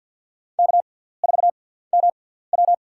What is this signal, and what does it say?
Morse code, heard as a keyed tone, about 53 wpm, spaced as for 17 wpm, 700 Hz. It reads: K4MW
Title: USA Call Signs - 6x Character Spacing - 50wpm